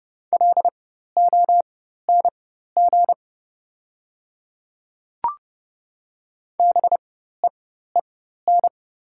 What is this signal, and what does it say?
Morse code, heard as a keyed tone, about 30 wpm, spaced as for 14 wpm, 700 Hz. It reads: LONG BEEN
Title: 4x Character Spacing - Top 300 Words - 30wpm